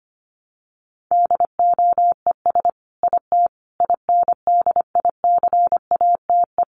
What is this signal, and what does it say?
Morse code, heard as a keyed tone, about 25 wpm, 700 Hz. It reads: DOES IT INDICATE
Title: Sentences from Top 1300 Words 25wpm